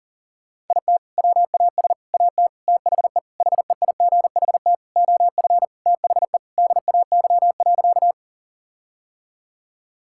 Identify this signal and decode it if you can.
Morse code, heard as a keyed tone, about 40 wpm, 700 Hz. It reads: IT WAS AT THE HEIGHT OF THE DAY.
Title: Sentences from Top 1600 Words 40wpm